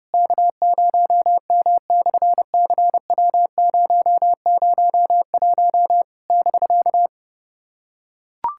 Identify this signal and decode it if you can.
Morse code, heard as a keyed tone, about 28 wpm, 700 Hz. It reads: K0M/CW001 <BK>
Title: SOTA QSO Practice 30wpm